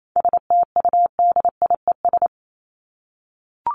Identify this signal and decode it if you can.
Morse code, heard as a keyed tone, about 28 wpm, 700 Hz. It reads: STUDIES
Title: ES:S Suffix - Top 100 Words with Base Words 28wpm